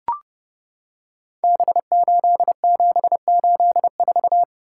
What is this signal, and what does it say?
Morse code, heard as a keyed tone, about 30 wpm, 700 Hz. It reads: B8784
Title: Code Groups - 5 Letter:Numbers - 30wpm